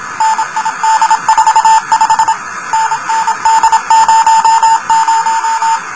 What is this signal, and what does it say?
Morse code, heard as a keyed tone, about 27 wpm, 910 Hz, with noise. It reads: NID45 NND00
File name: luxorion / Radio / rs12-23mar02-telemetry-cw